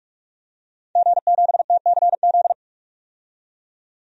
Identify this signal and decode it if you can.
Morse code, heard as a keyed tone, about 45 wpm, 700 Hz. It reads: G7TCZ
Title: DXCC-100 Call Signs - 5 Characters - 45wpm